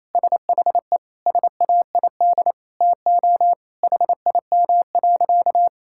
Morse code, heard as a keyed tone, about 28 wpm, 700 Hz. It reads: SHE SAID TO HIM.
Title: Sentences from Top 1600 Words 28wpm